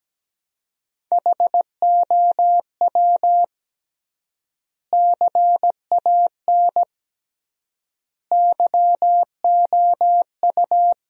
Morse code, heard as a keyed tone, about 17 wpm, 700 Hz. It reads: HOW CAN YOU
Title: 3x Word-Spacing - Sentences from Top 100 Words - 17wpm